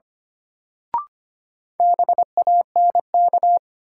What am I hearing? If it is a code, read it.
Morse code, heard as a keyed tone, about 25 wpm, 700 Hz. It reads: BANK
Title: Sentences from Top 900 Words - New Word Drill - 25wpm